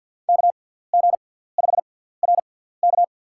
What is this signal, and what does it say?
Morse code, heard as a keyed tone, about 49 wpm, spaced as for 17 wpm, 700 Hz. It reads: KG5RK
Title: USA Call Signs - 5 Characters - 6x Character Spacing - 50wpm